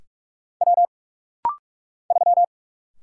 Morse code, heard as a keyed tone, about 44 wpm, 700 Hz. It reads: W 3
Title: Single Letter-Number Alt - Rapid-Fire 45wpm